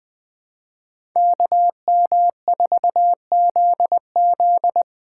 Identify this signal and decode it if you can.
Morse code, heard as a keyed tone, about 21 wpm, 700 Hz. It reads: KM4ZZ
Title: K1USN SST QSO Practice - 20wpm